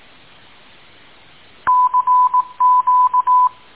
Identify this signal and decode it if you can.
Morse code, heard as a keyed tone, about 18 wpm, 1000 Hz. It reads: CQ